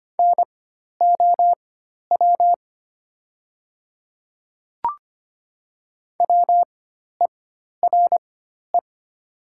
Morse code, heard as a keyed tone, about 25 wpm, spaced as for 12 wpm, 700 Hz. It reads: NOW WERE
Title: 4x Character Spacing - Top 100 Words - 25wpm